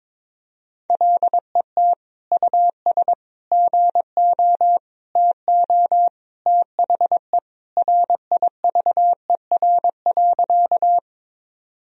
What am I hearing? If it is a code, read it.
Morse code, heard as a keyed tone, about 21 wpm, 700 Hz. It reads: LET US GO TO THE RIVER.
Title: Sentences from Top 1000 Words - Review Part 1 - 22wpm